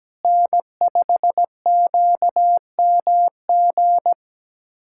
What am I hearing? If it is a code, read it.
Morse code, heard as a keyed tone, about 17 wpm, 700 Hz. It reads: N5QMG
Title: USA Callsign DE Callsign - 5 Characters - 17wpm